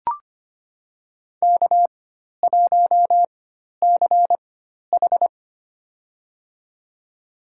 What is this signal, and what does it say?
Morse code, heard as a keyed tone, about 25 wpm, spaced as for 12 wpm, 700 Hz. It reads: K1CH
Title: USA Call Signs - 4x Character Spacing - 25wpm